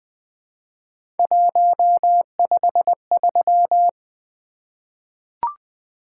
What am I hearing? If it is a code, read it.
Morse code, heard as a keyed tone, about 20 wpm, 700 Hz. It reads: 153
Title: Three Letter-Number Combinations - 20wpm